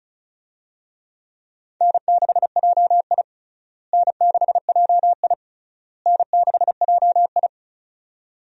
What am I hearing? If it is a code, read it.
Morse code, heard as a keyed tone, about 34 wpm, 700 Hz. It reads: N6JI N6JI N6JI
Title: USA Call Signs - 4 Characters - Repeated 3x - 35wpm